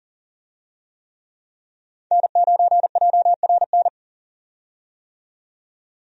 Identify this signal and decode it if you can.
Morse code, heard as a keyed tone, about 40 wpm, 700 Hz. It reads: N9JRN
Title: USA Call Signs - 5 Characters - 40wpm